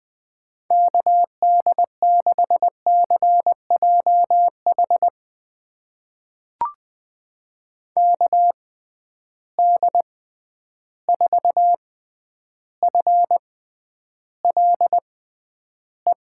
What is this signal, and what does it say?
Morse code, heard as a keyed tone, about 20 wpm, spaced as for 10 wpm, 700 Hz. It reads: KD6CJH KD4FLE
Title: USA Call Signs - 6 Characters - 6x Character Spacing - 20wpm